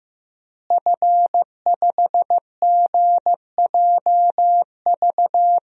Morse code, heard as a keyed tone, about 14 wpm, 700 Hz. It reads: F5GJV
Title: DXCC-100 Call Signs - 5 Characters - Repeated 3x - 15wpm